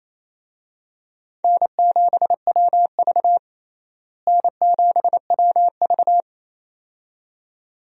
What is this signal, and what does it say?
Morse code, heard as a keyed tone, about 28 wpm, 700 Hz. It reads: N7WV N7WV
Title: USA Call Signs - 4 Characters - Repeated 2x - 28wpm